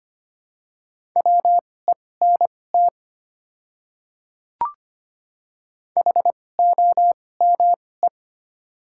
Morse code, heard as a keyed tone, about 25 wpm, spaced as for 18 wpm, 690 Hz. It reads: WENT HOME
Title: 2x Character Spacing - Top 200 Words - 25wpm